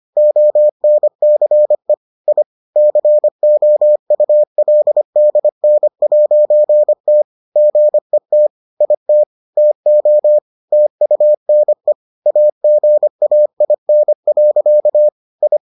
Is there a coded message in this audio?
Yes: Morse code from 0.2 to 15.6 s, heard as a keyed tone, about 25 wpm, 600 Hz. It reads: ONCE I COULDN'T GET IT TO TUNE AGAIN. I